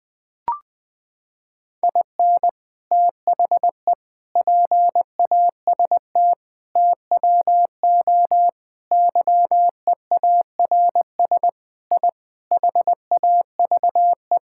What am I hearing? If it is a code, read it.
Morse code, heard as a keyed tone, about 20 wpm, 700 Hz. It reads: IN THE PAST TWO YEARS I HAVE
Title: Sentences with 5-Word Phrases - 20wpm